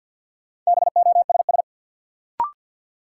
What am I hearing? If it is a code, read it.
Morse code, heard as a keyed tone, about 53 wpm, 700 Hz. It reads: BOSS
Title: Sentences from Top 700 Words - New Word Drill - 50wpm